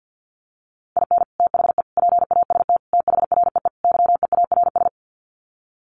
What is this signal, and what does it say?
Morse code, heard as a keyed tone, about 48 wpm, 700 Hz. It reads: IN THE PAST THREE YEARS
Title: Sets of 5 Words Encore2 50wpm